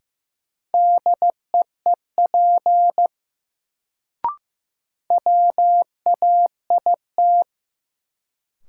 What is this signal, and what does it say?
Morse code, heard as a keyed tone, about 15 wpm, 700 Hz. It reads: DEEP WAIT
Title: Sets of 5 Words Encore - New Word Drill - 15wpm